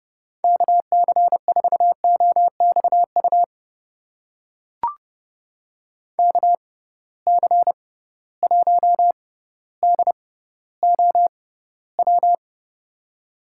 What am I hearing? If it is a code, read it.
Morse code, heard as a keyed tone, about 31 wpm, spaced as for 16 wpm, 700 Hz. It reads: KC4OXU KC1DOW
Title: USA Call Signs - 6 Characters - 6x Character Spacing - 30wpm